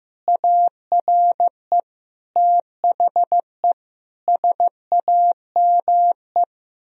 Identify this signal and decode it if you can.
Morse code, heard as a keyed tone, about 16 wpm, 700 Hz. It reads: ARE THE SAME
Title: Sentences from Top 1000 Words - Review Part 4 - 15wpm